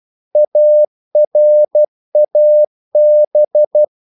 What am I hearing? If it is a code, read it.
Morse code, heard as a keyed tone, about 12 wpm, 600 Hz. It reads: ARAB
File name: CW / Practice / NationalCapitals12